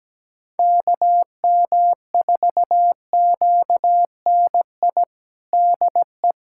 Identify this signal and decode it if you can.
Morse code, heard as a keyed tone, about 17 wpm, 700 Hz. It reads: KM4QNI DE
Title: USA Callsign DE Callsign - 6 Characters - 17wpm